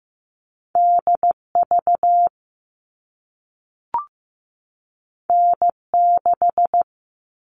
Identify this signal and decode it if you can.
Morse code, heard as a keyed tone, about 15 wpm, 700 Hz. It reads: DV N6
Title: All Cut Numbers - 2-Digit 15wpm